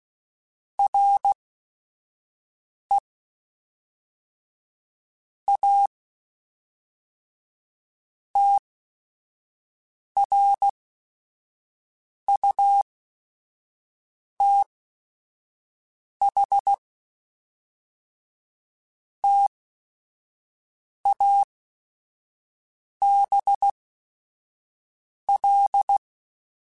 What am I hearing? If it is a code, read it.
Morse code, heard as a keyed tone, about 16 wpm, spaced as for 5 wpm, 800 Hz. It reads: RE A TRUTH TABL
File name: Technical Subjects / Technical Subjects-mp3s / AND Gate @5